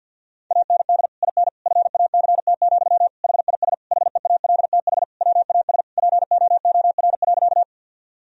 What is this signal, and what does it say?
Morse code, heard as a keyed tone, about 51 wpm, 700 Hz. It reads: AND IN FACT, HIS HEALTH WAS POOR.
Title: Sentences from Top 1000 Words - Review Part 3 - 50wpm